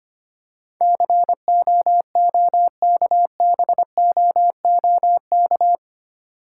Morse code, heard as a keyed tone, about 25 wpm, 700 Hz. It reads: COOKBOOK
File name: practice / 25wpm / Difficult Words with Hints - 25wpm